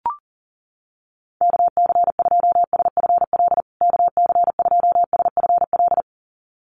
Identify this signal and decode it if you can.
Morse code, heard as a keyed tone, about 38 wpm, 700 Hz. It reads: KC2SFL KC2SFL
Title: USA Call Signs Repeated 2x 40wpm